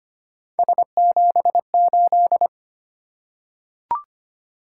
Morse code, heard as a keyed tone, about 25 wpm, 700 Hz. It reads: S78
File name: Three Letter-Number Combinations - 25wpm